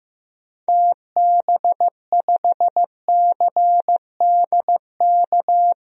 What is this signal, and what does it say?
Morse code, heard as a keyed tone, about 15 wpm, 700 Hz. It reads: TB5CDK
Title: DXCC-100 Call Signs - 6 Characters - Repeated 2x - 15wpm